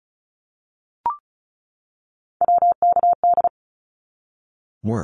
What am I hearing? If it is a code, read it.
Morse code, heard as a keyed tone, about 34 wpm, 700 Hz. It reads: WKD
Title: Abbreviations 35wpm